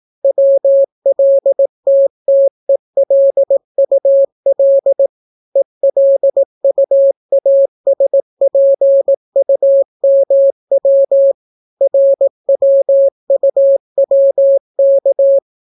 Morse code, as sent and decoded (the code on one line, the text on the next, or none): .-- .-.. - - . .-.. ..- .-.. / . .-.. ..- .- ... .--. ..- -- .-- / .-. .-- ..- .-- -.-
WLTTELUL ELUASPUMW RWUWK